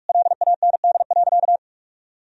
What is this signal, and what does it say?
Morse code, heard as a keyed tone, about 46 wpm, 700 Hz. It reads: LAND.
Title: Sentences from Top 1000 Words - Review Part 3 - 45wpm